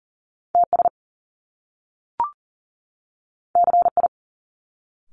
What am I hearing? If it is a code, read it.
Morse code, heard as a keyed tone, about 38 wpm, 700 Hz. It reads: TS CI
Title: Top Two Letter Combinations in Words 40wpm